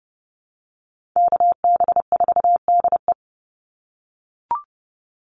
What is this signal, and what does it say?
Morse code, heard as a keyed tone, about 30 wpm, 700 Hz. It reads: KB4DE
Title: Call Signs - 30wpm